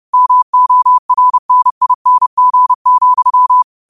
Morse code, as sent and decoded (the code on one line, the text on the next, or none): -- --- .-. -. .. -. --. --..--
MORNING,